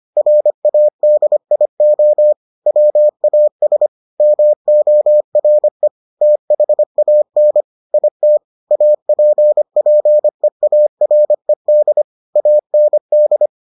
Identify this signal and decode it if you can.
Morse code, heard as a keyed tone, about 25 wpm, 600 Hz. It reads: RADIO WAS MORE THAN IT APPEARED AND